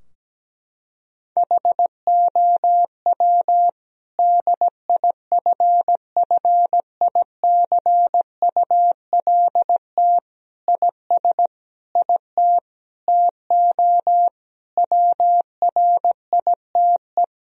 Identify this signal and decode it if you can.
Morse code, heard as a keyed tone, about 17 wpm, 700 Hz. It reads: HOW DIFFICULT IS IT TO WRITE
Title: Sentences from Top 1000 Words - Review Part 3 - 17wpm